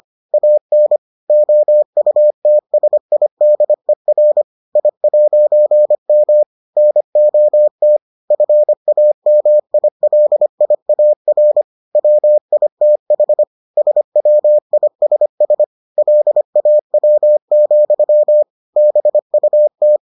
Morse code, heard as a keyed tone, about 25 wpm, 600 Hz. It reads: AN OUTSIDER I'M NOT FAMILIAR WITH SWISS LAW, BUT